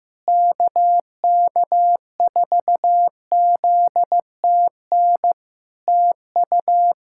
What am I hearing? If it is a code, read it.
Morse code, heard as a keyed tone, about 16 wpm, 700 Hz. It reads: KK4ZTN TU
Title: Practice QSO - Exchange 4 - Ending - 15wpm